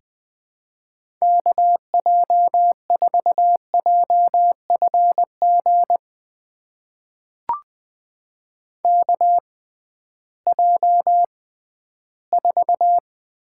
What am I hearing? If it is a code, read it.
Morse code, heard as a keyed tone, about 20 wpm, spaced as for 13 wpm, 700 Hz. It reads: KJ4JFG KJ4
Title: USA Call Signs - 6 Characters - 6x Character Spacing - 20wpm